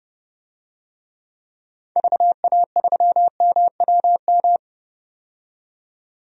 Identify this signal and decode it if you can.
Morse code, heard as a keyed tone, about 30 wpm, 700 Hz. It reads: VA3MWM
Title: DXCC-100 Call Signs - 6 Characters - 30wpm